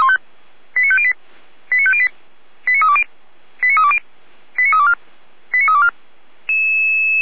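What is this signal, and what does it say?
Five-tone selective call (ZVEI), heard as quick runs of five tones, tones of about 70 ms. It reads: 89508 89508 89320 89320 89324 89324
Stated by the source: ZVEI five-tone call received by SDR, recorded by Feuerwehr Orlamünde.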